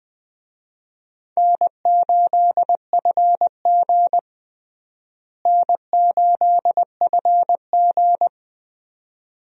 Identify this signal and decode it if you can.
Morse code, heard as a keyed tone, about 20 wpm, 700 Hz. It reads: N8FG N8FG
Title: USA Call Signs - 4 Characters - Repeated 2x - 20wpm